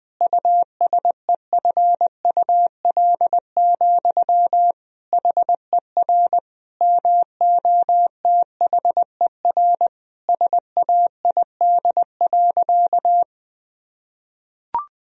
Morse code, heard as a keyed tone, about 20 wpm, 700 Hz. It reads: USEFUL, HER MOTHER SAID.
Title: Sentences from Top 1000 Words - Review Part 4 - 20wpm